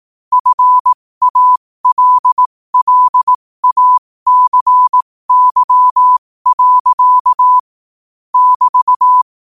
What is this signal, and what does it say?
Morse code, heard as a keyed tone, about 18 wpm, spaced as for 15 wpm, 1000 Hz. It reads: FALLACY. <BT>
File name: qotdfiles / 15wpm